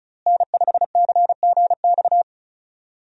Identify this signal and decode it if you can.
Morse code, heard as a keyed tone, about 34 wpm, 700 Hz. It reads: N5CGX
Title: USA Call Signs - 5 Characters - Repeated 3x - 35wpm